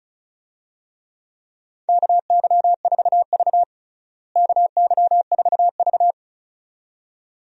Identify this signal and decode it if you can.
Morse code, heard as a keyed tone, about 36 wpm, 700 Hz. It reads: KY4V KY4V
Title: USA Call Signs - 4 Characters - Repeated 2x - 35wpm